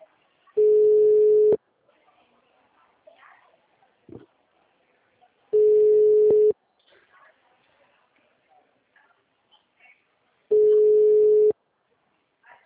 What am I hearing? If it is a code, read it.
Ringback tone (Europe, 425 Hz).
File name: out- -2010- - - .77190